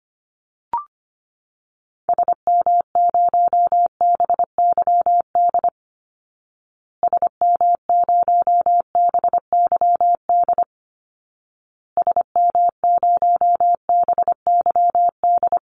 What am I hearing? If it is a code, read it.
Morse code, heard as a keyed tone, about 25 wpm, 700 Hz. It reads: SM0BYD SM0BYD SM0BYD
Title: Call Signs Repeated 2x - 25wpm